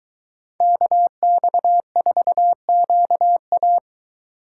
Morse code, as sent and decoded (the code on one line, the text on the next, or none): -.- -..- ....- --.- .-
KX4QA